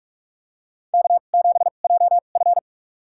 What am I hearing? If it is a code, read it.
Morse code, heard as a keyed tone, about 45 wpm, spaced as for 32 wpm, 700 Hz. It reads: K7JF
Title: USA Call Signs - 2x Character Spacing - 45wpm